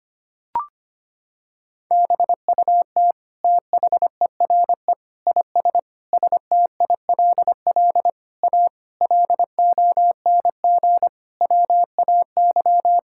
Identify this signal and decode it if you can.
Morse code, heard as a keyed tone, about 25 wpm, 700 Hz. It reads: BUT THERE IS STILL A LONG WAY
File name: Sets of 9 Words - 25wpm